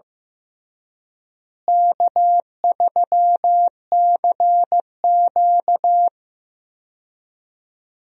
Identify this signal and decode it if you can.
Morse code, heard as a keyed tone, about 15 wpm, 700 Hz. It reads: K3CQ
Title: USA Call Signs - 4 Characters - Repeated 3x - 15wpm